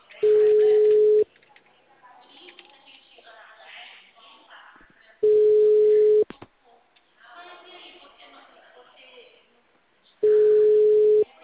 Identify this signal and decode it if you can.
Ringback tone (Europe, 425 Hz), with speech.